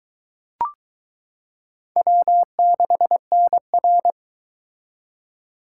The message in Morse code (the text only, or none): W6NR